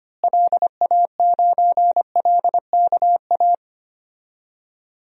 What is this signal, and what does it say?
Morse code, heard as a keyed tone, about 25 wpm, 700 Hz. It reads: LA9LKA
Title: DXCC-100 Call Signs - 6 Characters - Repeated 2x - 25wpm